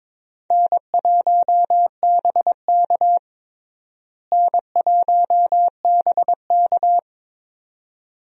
Morse code, heard as a keyed tone, about 22 wpm, 700 Hz. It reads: N1BK N1BK
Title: USA Call Signs - 4 Characters - Repeated 2x - 22wpm